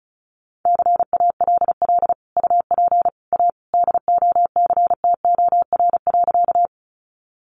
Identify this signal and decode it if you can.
Morse code, heard as a keyed tone, about 34 wpm, 700 Hz. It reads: CALL UP A DOCTOR.
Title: Sentences from Top 1500 Words 35wpm